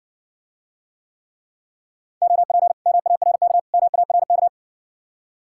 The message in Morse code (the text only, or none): OP MARK MARK